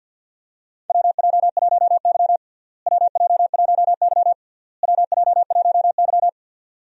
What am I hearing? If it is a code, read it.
Morse code, heard as a keyed tone, about 47 wpm, 700 Hz. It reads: WJ1Y WJ1Y WJ1Y
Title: USA Call Signs - 4 Characters - Repeated 3x - 50wpm